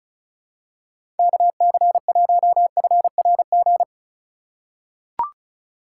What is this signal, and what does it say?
Morse code, heard as a keyed tone, about 36 wpm, 700 Hz. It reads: KC1FRG
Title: USA Call Signs - 6 Characters - 2x Character Spacing - 35wpm